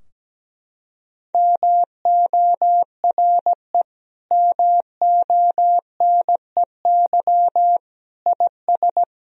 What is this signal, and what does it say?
Morse code, heard as a keyed tone, about 17 wpm, 700 Hz. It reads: MORE MONEY IS
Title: Sentences from Top 1000 Words - Review Part 3 - 17wpm